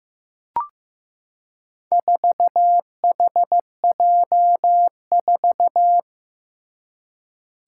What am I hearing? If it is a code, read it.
Morse code, heard as a keyed tone, about 15 wpm, 700 Hz. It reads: 4HJ4
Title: Four Challenging Letter-Number Combinations 15wpm